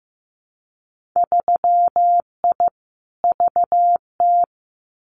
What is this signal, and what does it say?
Morse code, heard as a keyed tone, about 15 wpm, 700 Hz. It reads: 3I VT
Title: Winter Field Day Exchanges Encore 15wpm